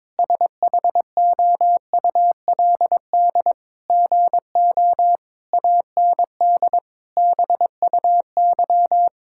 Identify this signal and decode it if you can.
Morse code, heard as a keyed tone, about 22 wpm, 700 Hz. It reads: SHOULD GO AND BUY
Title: Sentences from Top 1600 Words 22wpm